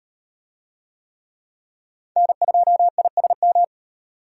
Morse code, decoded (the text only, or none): N2ISM